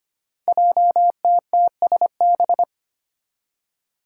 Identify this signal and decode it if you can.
Morse code, heard as a keyed tone, about 25 wpm, 700 Hz. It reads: JTTSB
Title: Five Challenging Letter-Number Combinations 25wpm